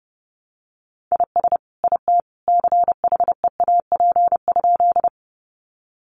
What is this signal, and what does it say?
Morse code, heard as a keyed tone, about 30 wpm, 700 Hz. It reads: IS IT CHEAP?
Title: Sentences from Top 1200 Words 30wpm